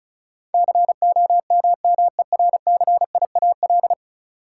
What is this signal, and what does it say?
Morse code, heard as a keyed tone, about 34 wpm, 700 Hz. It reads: COMMERCIAL
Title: Sentences from Top 1000 Words - Review Part 5 - New Word Drill - 35wpm